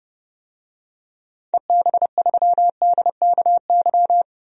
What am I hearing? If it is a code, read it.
Morse code, heard as a keyed tone, about 30 wpm, 700 Hz. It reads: EB3DKY
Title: DXCC-100 Call Signs - 6 Characters - 30wpm